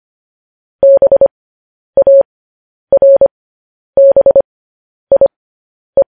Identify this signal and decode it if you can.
Morse code, heard as a keyed tone, about 25 wpm, spaced as for 10 wpm, 550 Hz. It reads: BARBIE
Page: news-headlines-in-morse-code-at-10-wpm